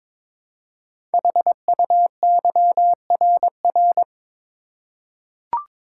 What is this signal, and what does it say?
Morse code, heard as a keyed tone, about 21 wpm, 700 Hz. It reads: HUYRR
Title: Code Groups - 5 Letters - 22wpm